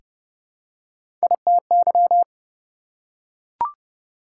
Morse code, heard as a keyed tone, about 30 wpm, 700 Hz. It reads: ITY
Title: Prefixes and Suffixes 30wpm